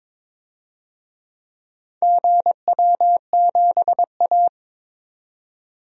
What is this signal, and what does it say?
Morse code, heard as a keyed tone, about 22 wpm, 700 Hz. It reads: GW7A
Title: DXCC-100 Call Signs - 4 Characters - 22wpm